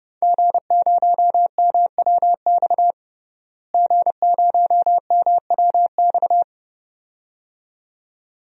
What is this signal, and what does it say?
Morse code, heard as a keyed tone, about 30 wpm, 700 Hz. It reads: G0MWX G0MWX
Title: DXCC-100 Call Signs - 5 Characters - Repeated 2x - 30wpm